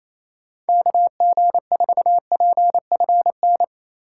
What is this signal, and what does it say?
Morse code, heard as a keyed tone, about 28 wpm, 700 Hz. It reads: KG4PFN
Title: USA Call Signs - 6 Characters - 4x Character Spacing - 28wpm